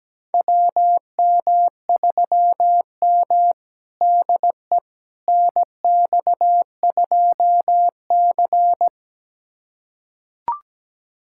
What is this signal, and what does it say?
Morse code, heard as a keyed tone, about 17 wpm, 700 Hz. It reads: WM3M DE NX2C
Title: USA Callsign DE Callsign - 4 Characters - 17wpm